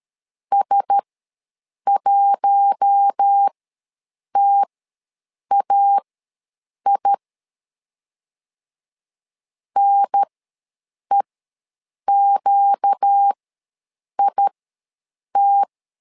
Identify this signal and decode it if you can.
Morse code, heard as a keyed tone, about 13 wpm, spaced as for 7 wpm, 800 Hz. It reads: S1TAI NEQIT